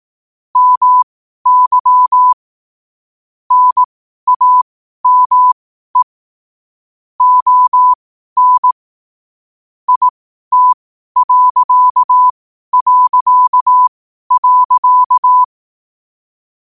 Morse code, heard as a keyed tone, about 18 wpm, spaced as for 12 wpm, 1000 Hz. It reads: MY NAME ON IT...
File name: qotdfiles / 12wpm